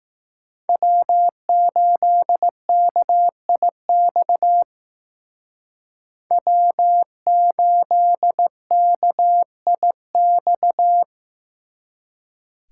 Morse code, heard as a keyed tone, about 18 wpm, 700 Hz. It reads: W8KIX W8KIX
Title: Speed Racing - USA 1x3 Call Signs 15wpm